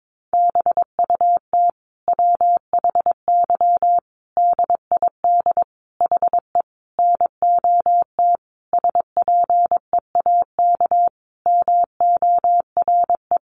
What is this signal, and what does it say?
Morse code, heard as a keyed tone, about 22 wpm, 700 Hz. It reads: BUT WHY DID HE NOT SPEAK MORE
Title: Sentences from Top 1300 Words 22wpm